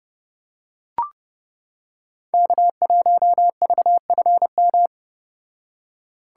Morse code, heard as a keyed tone, about 29 wpm, 700 Hz. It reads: K1VFM